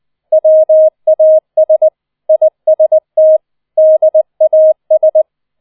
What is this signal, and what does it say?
Morse code, heard as a keyed tone, about 20 wpm, 630 Hz. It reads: WAS IST DAS